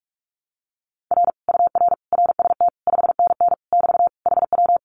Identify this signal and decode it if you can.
Morse code, heard as a keyed tone, about 46 wpm, 700 Hz. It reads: R UR RST 5NN <BT> HW
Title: Common QSO Phrases Encore 45wpm